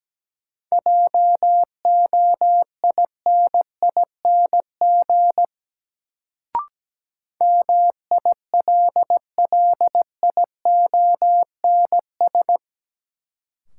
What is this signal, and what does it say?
Morse code, heard as a keyed tone, about 17 wpm, 700 Hz. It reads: JOINING MILLIONS
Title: Sets of 5 Words - New Word Drill - 17wpm